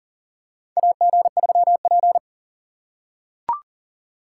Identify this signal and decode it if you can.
Morse code, heard as a keyed tone, about 40 wpm, 700 Hz. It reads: AG3P
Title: USA Call Signs - 4 Characters - 4x Character Spacing - 40wpm